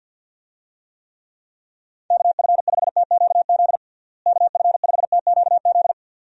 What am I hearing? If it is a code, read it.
Morse code, heard as a keyed tone, about 52 wpm, 700 Hz. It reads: KF5TQZ KF5TQZ
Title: USA Call Signs - 6 Characters - Repeated 2x - 50wpm